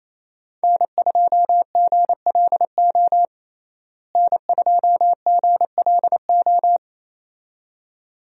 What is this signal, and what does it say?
Morse code, heard as a keyed tone, about 28 wpm, 700 Hz. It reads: N2GLO N2GLO
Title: USA Call Signs - 5 Characters - Repeated 2x - 28wpm